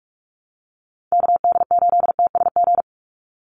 Morse code, heard as a keyed tone, about 44 wpm, 700 Hz. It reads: KD8TSG
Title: USA Call Signs 45wpm